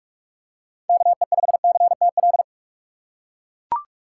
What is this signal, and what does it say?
Morse code, heard as a keyed tone, about 42 wpm, 700 Hz. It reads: KE5CTL
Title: USA Call Signs - 6 Characters - 3x Character Spacing - 45wpm